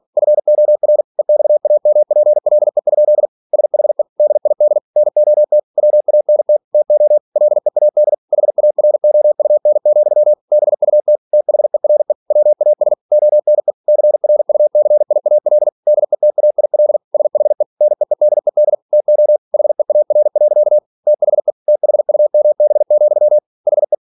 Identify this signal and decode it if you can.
Morse code, heard as a keyed tone, about 46 wpm, 600 Hz. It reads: FOR EXAMPLE? SHE DID NOT WANT TO LEAD HAROUN, BUT THERE WAS ONE CRUCIAL DETAIL SHE NEEDED TO HEAR. THE THUMB, HE
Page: Never 47wpm